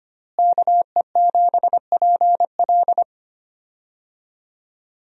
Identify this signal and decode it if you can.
Morse code, heard as a keyed tone, about 25 wpm, 700 Hz. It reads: KE7PL
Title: USA Call Signs - 5 Characters - Repeated 2x - 25wpm